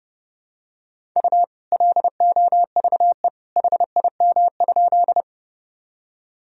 Morse code, heard as a keyed tone, about 32 wpm, 700 Hz. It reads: U LOVE HIM?
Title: Sentences from Top 400 Words and Abbreviations 30wpm